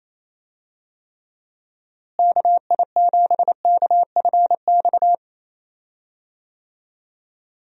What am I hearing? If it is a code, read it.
Morse code, heard as a keyed tone, about 28 wpm, 700 Hz. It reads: KI7KFX
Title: USA Call Signs - 6 Characters - Repeated 3x - 28wpm